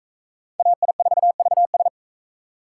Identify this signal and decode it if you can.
Morse code, heard as a keyed tone, about 41 wpm, 700 Hz. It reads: AI4VS